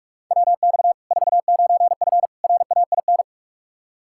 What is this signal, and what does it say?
Morse code, heard as a keyed tone, about 43 wpm, 700 Hz. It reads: WX 49F RAIN
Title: Practice QSO - Exchange 2 - Info - 45wpm